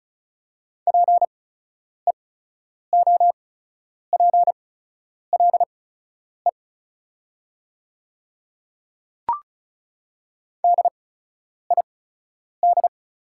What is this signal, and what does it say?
Morse code, heard as a keyed tone, about 36 wpm, spaced as for 10 wpm, 690 Hz. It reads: PEOPLE DID
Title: 8x Character Spacing - Top 100 Words - 35wpm